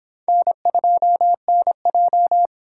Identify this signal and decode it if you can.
Morse code, heard as a keyed tone, about 26 wpm, 700 Hz. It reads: N2NJ